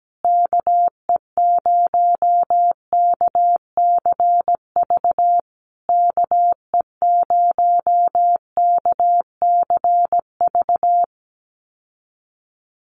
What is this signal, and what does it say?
Morse code, heard as a keyed tone, about 17 wpm, 700 Hz. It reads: KE0KCV KE0KCV
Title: USA Call Signs Repeated 3x 17wpm